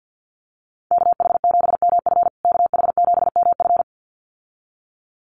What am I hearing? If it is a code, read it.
Morse code, heard as a keyed tone, about 50 wpm, 700 Hz. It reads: KH7MF KH7MF
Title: USA Call Signs Repeated 2x 50wpm